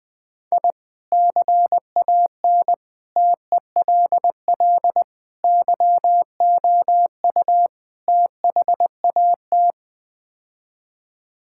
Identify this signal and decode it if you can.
Morse code, heard as a keyed tone, about 21 wpm, 700 Hz. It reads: I CAN TELL YOU THAT
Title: Build A Phrase - 5-Word Phrases - 20wpm